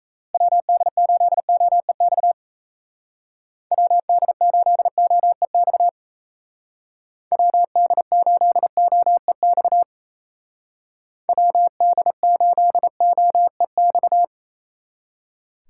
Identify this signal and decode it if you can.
Morse code, heard as a keyed tone, about 37 wpm, 700 Hz. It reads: WD8OEX WD8OEX WD8OEX WD8OEX